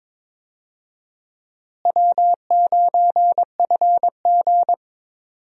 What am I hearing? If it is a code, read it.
Morse code, heard as a keyed tone, about 22 wpm, 700 Hz. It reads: W9FG
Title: USA Call Signs - 4 Characters - Repeated 2x - 22wpm